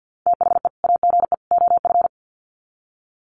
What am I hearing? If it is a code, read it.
Morse code, heard as a keyed tone, about 50 wpm, 700 Hz. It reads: THE AGE OF
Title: Sets of 3 Words 50wpm